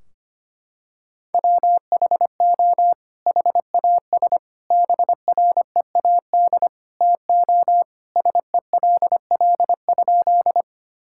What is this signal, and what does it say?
Morse code, heard as a keyed tone, about 25 wpm, 700 Hz. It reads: WHO HAS BREAD TO SELL?
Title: Sentences from Top 1600 Words 25wpm